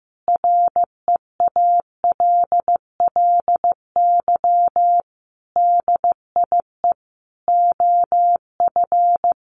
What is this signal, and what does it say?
Morse code, heard as a keyed tone, about 15 wpm, 700 Hz. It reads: REALLY DIE OF
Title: Sentences from Top 1300 Words 15wpm